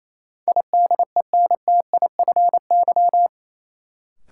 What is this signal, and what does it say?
Morse code, heard as a keyed tone, about 28 wpm, 700 Hz. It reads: IDENTIFY
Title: Sentences from Top 900 Words - New Word Drill - 28wpm